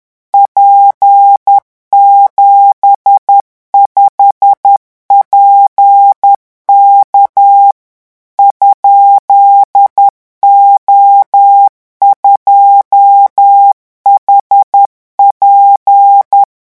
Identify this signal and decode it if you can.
Morse code, heard as a keyed tone, about 10 wpm, 800 Hz. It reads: P75PK ?O2HP
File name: morse mp3 / revision13-vitesse 10 mots